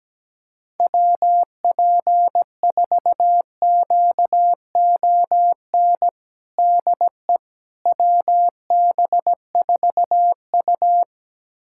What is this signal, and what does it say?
Morse code, heard as a keyed tone, about 17 wpm, 700 Hz. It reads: WP4QON DE WB4U
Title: USA Callsign DE Callsign 17wpm